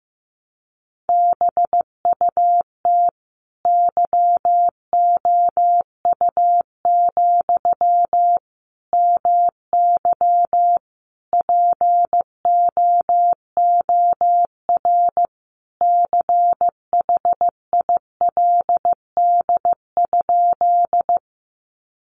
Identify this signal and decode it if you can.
Morse code, heard as a keyed tone, about 15 wpm, 700 Hz. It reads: BUT YOU, MY POOR CHILD?
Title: Sentences from Top 1000 Words 15wpm